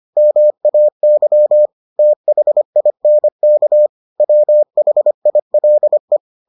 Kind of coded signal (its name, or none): Morse code